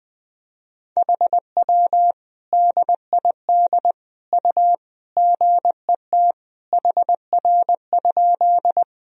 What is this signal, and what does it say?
Morse code, heard as a keyed tone, about 21 wpm, 700 Hz. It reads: HW DID U GET HR?
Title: Sentences from Top 200 Words and Abbreviations 20wpm